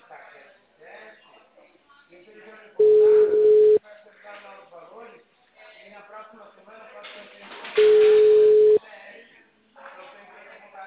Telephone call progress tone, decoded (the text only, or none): ringback tone